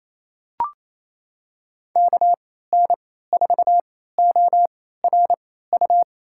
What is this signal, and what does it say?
Morse code, heard as a keyed tone, about 28 wpm, spaced as for 16 wpm, 700 Hz. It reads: KN4ORU
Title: USA Call Signs - 6 Characters - 3x Character Spacing - 28wpm